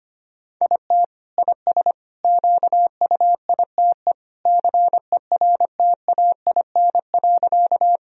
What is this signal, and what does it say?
Morse code, heard as a keyed tone, about 25 wpm, 700 Hz. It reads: IT IS QUITE CERTAIN.
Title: Sentences from Top 1000 Words - Review Part 2 - 25wpm